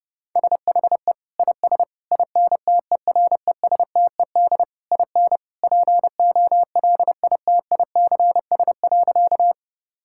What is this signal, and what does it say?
Morse code, heard as a keyed tone, about 30 wpm, 700 Hz. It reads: SHE IS INTERESTED IN POLITICS.
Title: Sentences from Top 1000 Words - Review Part 6 - 30wpm